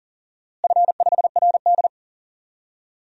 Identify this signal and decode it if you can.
Morse code, heard as a keyed tone, about 40 wpm, 700 Hz. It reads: F5RD
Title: DXCC-100 Call Signs - 4 Characters - Repeated 2x - 40wpm